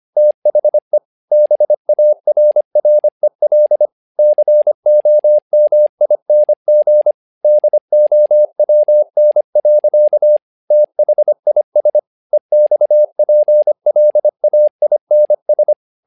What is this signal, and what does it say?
Morse code, heard as a keyed tone, about 25 wpm, 600 Hz. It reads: THE BARREL COMING DOWN. THIS EXPLAINS
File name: learning center / tower trouble / tower trouble 25wpm